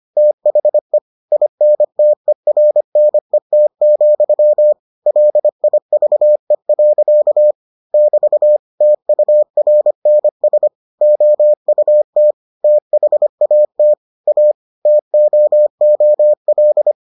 Morse code, heard as a keyed tone, about 25 wpm, 600 Hz. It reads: THE INTERNET, LIVE. <BT> TURNS OUT THAT A TOOL